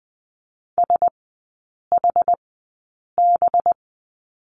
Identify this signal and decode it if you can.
Morse code, heard as a keyed tone, about 20 wpm, spaced as for 8 wpm, 700 Hz. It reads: SHB
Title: 184 Lesson 30e - Letter b - Three Characters 20wpm